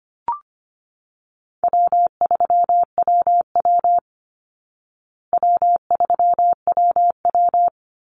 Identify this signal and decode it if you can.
Morse code, heard as a keyed tone, about 25 wpm, 700 Hz. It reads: W3WW W3WW
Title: Easy Call Signs Repeated 2x - 25wpm